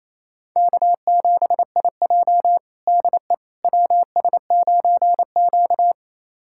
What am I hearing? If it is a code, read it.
Morse code, heard as a keyed tone, about 28 wpm, 700 Hz. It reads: K7IJ DE WS9Q
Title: USA Callsign DE Callsign - 4 Characters - 28wpm